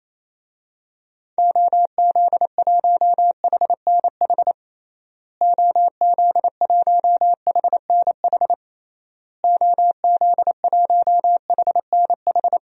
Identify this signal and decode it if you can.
Morse code, heard as a keyed tone, about 28 wpm, 700 Hz. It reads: OZ1HNH OZ1HNH OZ1HNH
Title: DXCC-100 Call Signs - 6 Characters - Repeated 3x - 28wpm